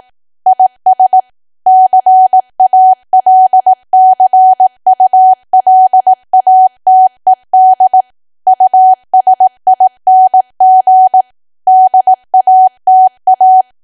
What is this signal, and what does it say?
Morse code, heard as a keyed tone, about 18 wpm, 750 Hz. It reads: IS CALCULATED USING DATA
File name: cwa / W1AW-Level2 / 18WPM